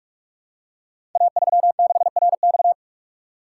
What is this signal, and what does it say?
Morse code, heard as a keyed tone, about 44 wpm, 690 Hz. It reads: A36RX
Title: Code Groups - 5 Letter:Numbers - 45wpm